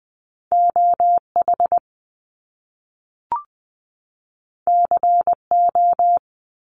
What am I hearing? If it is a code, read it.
Morse code, heard as a keyed tone, about 20 wpm, 700 Hz. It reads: OH CO
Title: US States 20wpm